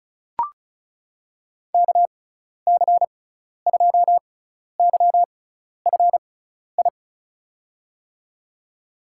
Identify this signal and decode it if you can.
Morse code, heard as a keyed tone, about 34 wpm, spaced as for 12 wpm, 700 Hz. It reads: KC2YFI
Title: USA Call Signs - 6 Characters - 6x Character Spacing - 35wpm